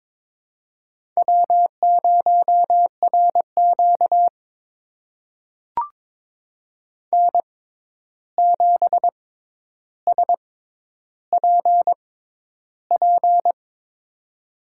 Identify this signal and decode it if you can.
Morse code, heard as a keyed tone, about 22 wpm, spaced as for 11 wpm, 700 Hz. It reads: W0RQ N7SPP
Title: USA Call Signs - 6x Character Spacing - 22wpm